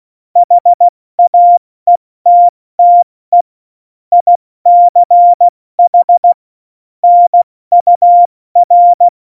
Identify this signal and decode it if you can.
Morse code, heard as a keyed tone, about 16 wpm, 700 Hz. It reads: HAETTE ICH NUR
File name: Morsekurs / Witze / Witz 29. September 2020, 70 F80